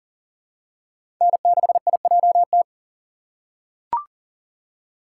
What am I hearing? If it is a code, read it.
Morse code, heard as a keyed tone, about 40 wpm, 700 Hz. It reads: N6IJT
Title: USA Call Signs - 5 Characters - 3x Character Spacing - 40wpm